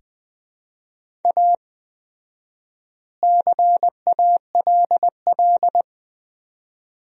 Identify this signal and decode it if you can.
Morse code, heard as a keyed tone, about 20 wpm, 700 Hz. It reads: A CALL
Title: 4x Word-Spacing - Sentences from Top 300 Words - 20wpm